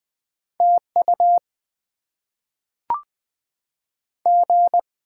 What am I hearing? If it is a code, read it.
Morse code, heard as a keyed tone, about 20 wpm, 700 Hz. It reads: TU G
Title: Top 20 QSO Elements - 8x Character Spacing - 20wpm